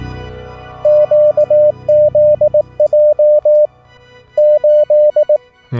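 Morse code, heard as a keyed tone, about 18 wpm, 600 Hz. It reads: QZJ 8
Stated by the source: Swedish Morse lesson on the format of a CW QSO and common traffic abbreviations and Q-codes.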